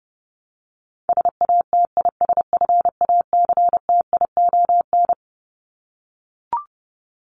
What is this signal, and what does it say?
Morse code, heard as a keyed tone, about 30 wpm, 700 Hz. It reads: SATISFACTION
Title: Top 1751-2000 Words 30wpm